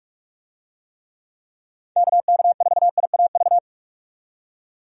Morse code, heard as a keyed tone, about 46 wpm, 700 Hz. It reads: KK4IAV
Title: USA Call Signs - 6 Characters - 45wpm